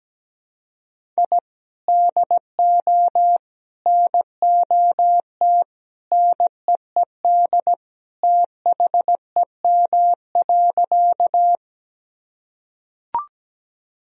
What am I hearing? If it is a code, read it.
Morse code, heard as a keyed tone, about 17 wpm, 700 Hz. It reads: I DO NOT NEED THEM.
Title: Sentences from Top 1000 Words - Review Part 1 - 17wpm